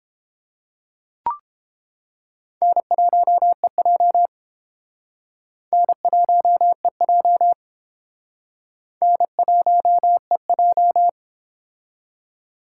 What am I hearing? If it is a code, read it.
Morse code, heard as a keyed tone, about 28 wpm, 700 Hz. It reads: N1EJ N1EJ N1EJ